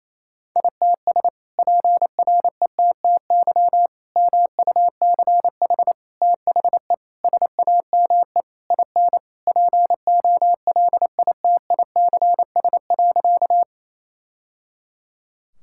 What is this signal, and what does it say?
Morse code, heard as a keyed tone, about 28 wpm, 700 Hz. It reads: ITS PRETTY MUCH THE SAME IN POLITICS.
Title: Sentences from Top 1000 Words - Review Part 6 - 28wpm